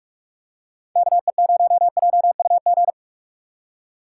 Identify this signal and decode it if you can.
Morse code, heard as a keyed tone, about 46 wpm, 690 Hz. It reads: KE0JUG
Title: USA Call Signs - 6 Characters - 3x Character Spacing - 45wpm